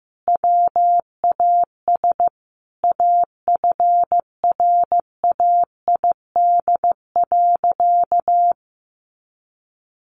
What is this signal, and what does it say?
Morse code, heard as a keyed tone, about 15 wpm, 700 Hz. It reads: WAS AFRAID.
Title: Sentences from Top 1500 Words 15wpm